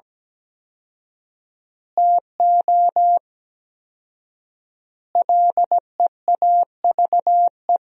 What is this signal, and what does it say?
Morse code, heard as a keyed tone, about 17 wpm, 700 Hz. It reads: TO LEAVE
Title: 4x Word-Spacing - Sentences from Top 400 Words - 17wpm